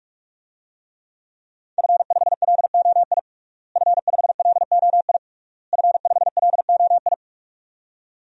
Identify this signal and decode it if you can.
Morse code, heard as a keyed tone, about 43 wpm, 700 Hz. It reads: F5LOI F5LOI F5LOI
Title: DXCC-100 Call Signs - 5 Characters - Repeated 3x - 45wpm